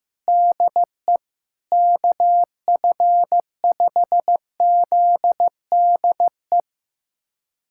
Morse code, heard as a keyed tone, about 15 wpm, 700 Hz. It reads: DE KF5ZDE
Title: USA Callsign DE Callsign 15wpm